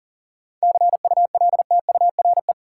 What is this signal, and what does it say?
Morse code, heard as a keyed tone, about 40 wpm, 700 Hz. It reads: CULTURE
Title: Sentences from Top 1000 Words - Review Part 6 - New Word Drill - 40wpm